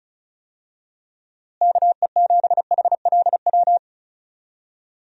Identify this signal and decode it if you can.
Morse code, heard as a keyed tone, about 36 wpm, 700 Hz. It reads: KE7HLW
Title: USA Call Signs - 6 Characters - Repeated 2x - 35wpm